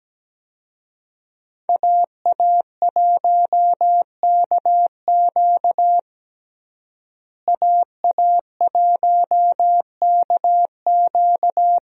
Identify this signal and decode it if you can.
Morse code, heard as a keyed tone, about 17 wpm, 700 Hz. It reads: AA1KQ AA1KQ
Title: USA Call Signs - 5 Characters - Repeated 3x - 17wpm